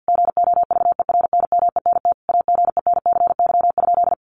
Morse code, heard as a keyed tone, about 50 wpm, 700 Hz. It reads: GOVERNMENT AGENCY?
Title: Sentences from Top 1300 Words 50wpm